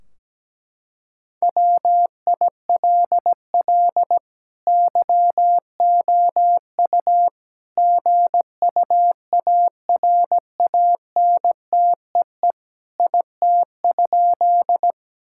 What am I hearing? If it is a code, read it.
Morse code, heard as a keyed tone, about 17 wpm, 700 Hz. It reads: WILL YOU GUARANTEE IT?
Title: Sentences from Top 1600 Words 17wpm